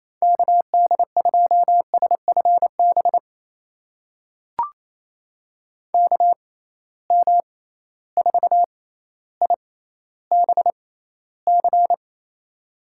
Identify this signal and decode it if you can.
Morse code, heard as a keyed tone, about 28 wpm, spaced as for 14 wpm, 700 Hz. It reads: KD2SFB KM4IBC
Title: USA Call Signs - 6 Characters - 6x Character Spacing - 28wpm